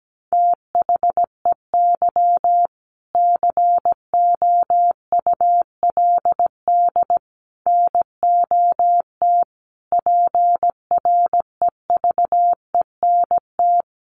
Morse code, heard as a keyed tone, about 17 wpm, 700 Hz. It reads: THEY COULD NOT PREVENT
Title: Sentences from Top 1100 Words 17wpm